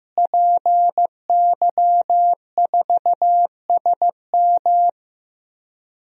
Morse code, heard as a keyed tone, about 15 wpm, 700 Hz. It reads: PY4SM
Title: DXCC-100 Call Signs - 5 Characters - Repeated 3x - 15wpm